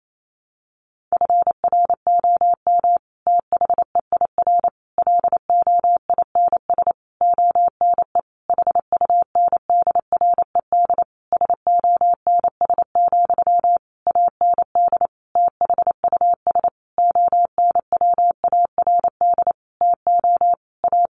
Morse code, heard as a keyed tone, about 28 wpm, 700 Hz. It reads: FROM THEIR LOINS ONE HUNDRED SONS, AND THUS ONWARD TO A
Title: A Sound Of Thunder 28wpm